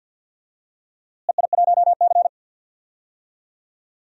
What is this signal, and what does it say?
Morse code, heard as a keyed tone, about 51 wpm, 700 Hz. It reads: EI1C
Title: DXCC-100 Call Signs - 4 Characters - Repeated 3x - 50wpm